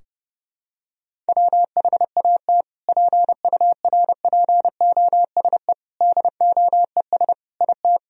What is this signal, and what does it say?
Morse code, heard as a keyed tone, about 32 wpm, 700 Hz. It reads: WHAT PURPOSE DOES IT
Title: Sentences from Top 1000 Words - Review Part 2 - 30wpm